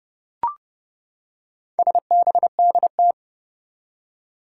Morse code, heard as a keyed tone, about 30 wpm, 700 Hz. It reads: SBDT
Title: Four Challenging Letter-Number Combinations 30wpm